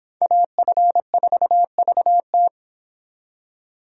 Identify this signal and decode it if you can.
Morse code, heard as a keyed tone, about 29 wpm, 700 Hz. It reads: AF4VT